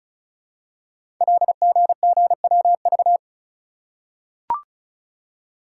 Morse code, heard as a keyed tone, about 36 wpm, 700 Hz. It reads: LGGWV
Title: Code Groups - 5 Letters - 35wpm